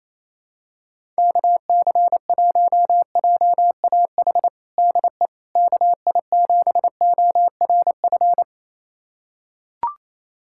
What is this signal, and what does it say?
Morse code, heard as a keyed tone, about 28 wpm, 700 Hz. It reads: KC1JAH DE KI7ORF
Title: USA Callsign DE Callsign 28wpm